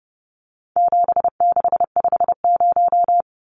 Morse code, heard as a keyed tone, about 30 wpm, 700 Hz. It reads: 7650
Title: Numbers - 4-Digit 30wpm